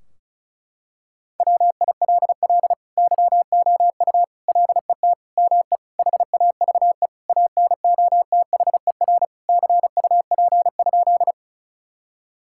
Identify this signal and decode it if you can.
Morse code, heard as a keyed tone, about 36 wpm, 700 Hz. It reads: WILL YOU LET ME HAVE ANOTHER CUP?
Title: Sentences from Top 1600 Words 35wpm